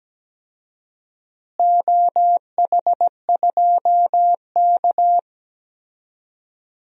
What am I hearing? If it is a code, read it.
Morse code, heard as a keyed tone, about 17 wpm, 700 Hz. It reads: OH2K
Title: DXCC-100 Call Signs - 4 Characters - Repeated 2x - 17wpm